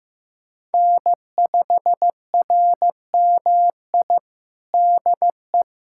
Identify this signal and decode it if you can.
Morse code, heard as a keyed tone, about 15 wpm, 700 Hz. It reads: N5RMI DE
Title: USA Callsign DE Callsign - 5 Characters - 15wpm